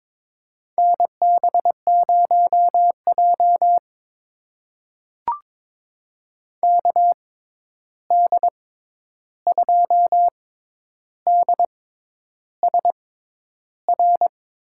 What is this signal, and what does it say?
Morse code, heard as a keyed tone, about 22 wpm, spaced as for 10 wpm, 700 Hz. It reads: NB0J KD2DSR
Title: USA Call Signs - 6x Character Spacing - 22wpm